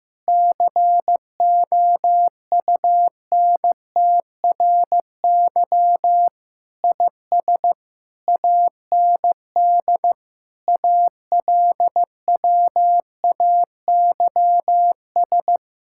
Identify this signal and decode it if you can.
Morse code, heard as a keyed tone, about 16 wpm, 700 Hz. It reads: COUNTRY IS AND ALWAYS
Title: Sentences from Top 1000 Words - Review Part 6 - 15wpm